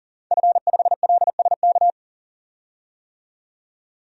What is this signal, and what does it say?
Morse code, heard as a keyed tone, about 40 wpm, 700 Hz. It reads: F5LSK
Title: DXCC-100 Call Signs - 5 Characters - 40wpm